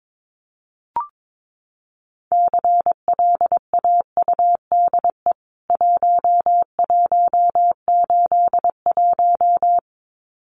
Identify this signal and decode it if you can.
Morse code, heard as a keyed tone, about 22 wpm, 700 Hz. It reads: CLAUDE 1181
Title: CWT Exchange 22wpm